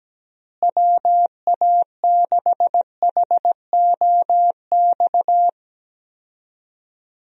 Morse code, heard as a keyed tone, about 17 wpm, 700 Hz. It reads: WA6HOX
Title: USA Call Signs - 6 Characters - Repeated 2x - 17wpm